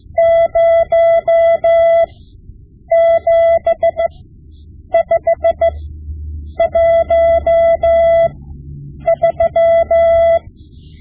Morse code, heard as a keyed tone, about 15 wpm, spaced as for 8 wpm, 670 Hz. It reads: 07513
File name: Morse Lessons / Morse Lessons / Tape-5B